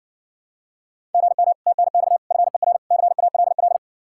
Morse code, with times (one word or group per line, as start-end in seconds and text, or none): 1.1-1.5 GM
1.7-2.2 TNX
2.3-2.8 FER
2.9-3.8 CALL